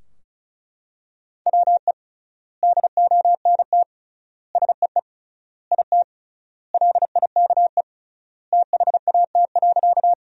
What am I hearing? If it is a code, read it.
Morse code, heard as a keyed tone, about 36 wpm, 700 Hz. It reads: WE DONT SEE IT LIKE THAT.
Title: 3x Word-Spacing - Sentences from Top 300 Words - 35wpm